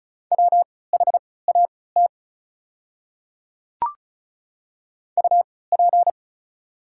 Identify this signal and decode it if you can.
Morse code, heard as a keyed tone, about 36 wpm, spaced as for 20 wpm, 700 Hz. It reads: WHAT UP
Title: 3x Character Spacing - Top 100 Words - 35wpm